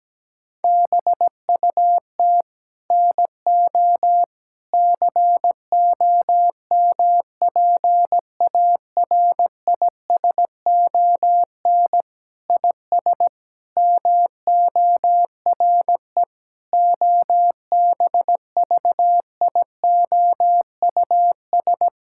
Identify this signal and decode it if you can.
Morse code, heard as a keyed tone, about 17 wpm, 700 Hz. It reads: BUT NO COMPARISON IS MORE OBVIOUS
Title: Sentences from Top 1600 Words 17wpm